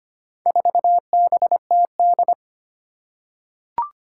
Morse code, heard as a keyed tone, about 25 wpm, 700 Hz. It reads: 4BTD
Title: Four Challenging Letter-Number Combinations 25wpm